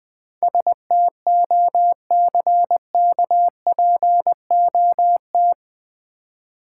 Morse code, heard as a keyed tone, about 20 wpm, 700 Hz. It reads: STOCKPOT
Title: Easy Words - Speed-Racing with Hints - 20wpm